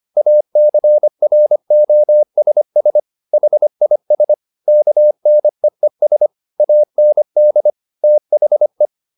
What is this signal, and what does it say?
Morse code, heard as a keyed tone, about 25 wpm, 600 Hz. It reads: ACROSS HIS KNEES AND THE
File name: the old man and the sea part 1 25wpm